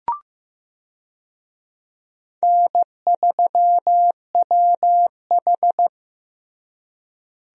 Morse code, heard as a keyed tone, about 15 wpm, 700 Hz. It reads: N3WH
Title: USA Call Signs - 4 Characters - Repeated 3x - 15wpm